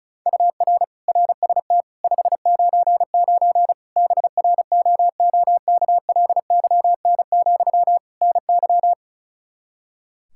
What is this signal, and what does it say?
Morse code, heard as a keyed tone, about 36 wpm, 700 Hz. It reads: UR RST 599 BROOKLYN, NY
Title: Practice QSO - Exchange 1 - Beginning - 35wpm